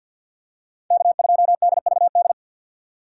Morse code, heard as a keyed tone, about 52 wpm, 700 Hz. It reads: K2DVD
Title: USA Call Signs - 5 Characters - 3x Character Spacing - 50wpm